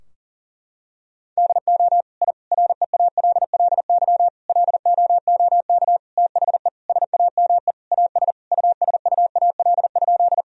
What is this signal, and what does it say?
Morse code, heard as a keyed tone, about 40 wpm, 700 Hz. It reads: DO I REALLY LOOK THE SAME AS USUAL?
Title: Sentences from Top 1000 Words - Review Part 2 - 40wpm